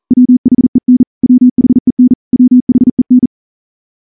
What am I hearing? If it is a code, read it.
Morse code, heard as a keyed tone, about 38 wpm, 250 Hz. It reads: WHEN WHEN WHEN
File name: morse-training-40WPM